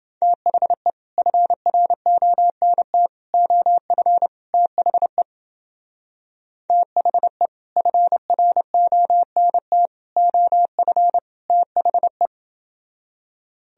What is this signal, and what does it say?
Morse code, heard as a keyed tone, about 28 wpm, 700 Hz. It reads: THE FRONT OF THE THE FRONT OF THE